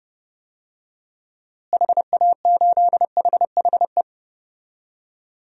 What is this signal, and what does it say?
Morse code, heard as a keyed tone, about 29 wpm, 700 Hz. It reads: HA8HHE
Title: DXCC-100 Call Signs - 6 Characters - Repeated 2x - 30wpm